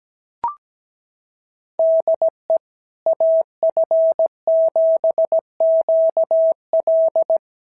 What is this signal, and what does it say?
Morse code, heard as a keyed tone, about 17 wpm, 650 Hz. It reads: DE AF7QL
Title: SOTA QSO Practice 17wpm